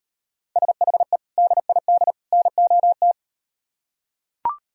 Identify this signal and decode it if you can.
Morse code, heard as a keyed tone, about 38 wpm, 700 Hz. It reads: SHE DID NOT